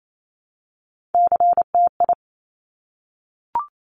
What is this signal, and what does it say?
Morse code, heard as a keyed tone, about 28 wpm, 700 Hz. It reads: CTI
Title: Top Three Letter Combinations in Words 28wpm